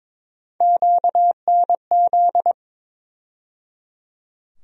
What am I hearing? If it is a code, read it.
Morse code, heard as a keyed tone, about 22 wpm, 700 Hz. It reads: QNZ
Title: USA Call Sign Postfixes - 3 Letters 22wpm